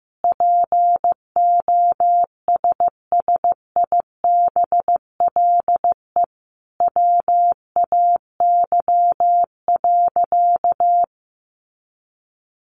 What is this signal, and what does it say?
Morse code, heard as a keyed tone, about 15 wpm, 700 Hz. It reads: POSSIBLE WAY.
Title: Sentences from Top 500 Words 15wpm